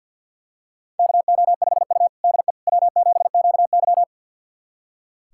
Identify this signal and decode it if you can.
Morse code, heard as a keyed tone, about 46 wpm, 700 Hz. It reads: KO5U DE W7QY
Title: USA Callsign DE Callsign - 4 Characters - 50wpm